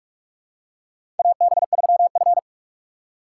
Morse code, heard as a keyed tone, about 47 wpm, 700 Hz. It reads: AB3F
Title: USA Call Signs - 4 Characters - 2x Character Spacing - 45wpm